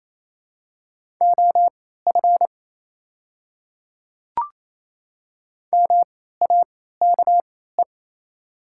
Morse code, heard as a keyed tone, about 27 wpm, spaced as for 16 wpm, 700 Hz. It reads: OF MAKE
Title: 3x Character Spacing - Top 500 Words - 28wpm